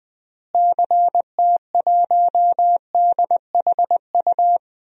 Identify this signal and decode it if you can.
Morse code, heard as a keyed tone, about 20 wpm, 700 Hz. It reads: CT1DHU
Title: DXCC-100 Call Signs - 6 Characters - Repeated 3x - 20wpm